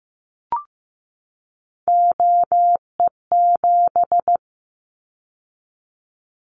Morse code, heard as a keyed tone, about 15 wpm, 700 Hz. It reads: OE7
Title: Callsign Prefixes for Top 50 DXCC - 15wpm